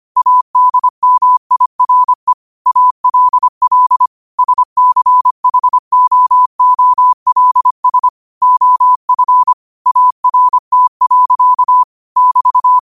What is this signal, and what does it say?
Morse code, heard as a keyed tone, about 25 wpm, 1000 Hz. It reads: ADMIRE ALL SCHOOLS OF ART. <BT>
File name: qotdfiles / 25wpm